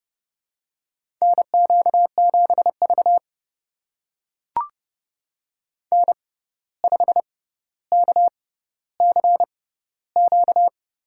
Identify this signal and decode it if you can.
Morse code, heard as a keyed tone, about 31 wpm, spaced as for 14 wpm, 700 Hz. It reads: NQ7V N5KCQ
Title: USA Call Signs - 6x Character Spacing - 30wpm